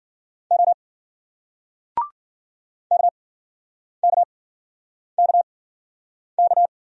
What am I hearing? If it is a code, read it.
Morse code, heard as a keyed tone, about 54 wpm, spaced as for 9 wpm, 700 Hz. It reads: O KKKK